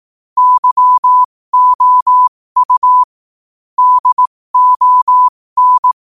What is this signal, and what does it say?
Morse code, heard as a keyed tone, about 18 wpm, spaced as for 15 wpm, 1000 Hz. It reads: YOU DON
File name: qotdfiles / 15wpm